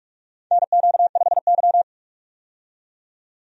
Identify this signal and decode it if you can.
Morse code, heard as a keyed tone, about 46 wpm, 700 Hz. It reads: NQ5Y
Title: USA Call Signs - 4 Characters - 45wpm